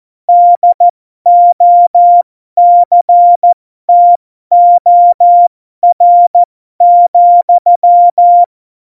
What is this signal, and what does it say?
Morse code, heard as a keyed tone, about 14 wpm, spaced as for 12 wpm, 700 Hz. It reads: DOCTOR,
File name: Morsekurs / Witze